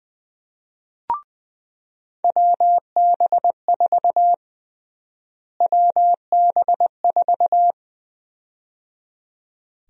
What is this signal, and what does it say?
Morse code, heard as a keyed tone, about 20 wpm, 700 Hz. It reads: WB4 WB4
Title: USA Call Sign Prefixes - Repeated 2x - 20wpm